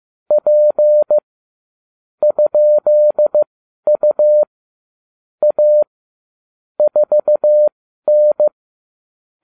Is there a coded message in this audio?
Yes: Morse code from 0.3 to 8.5 s, heard as a keyed tone, about 15 wpm, spaced as for 12 wpm, 600 Hz. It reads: P ?U A 4N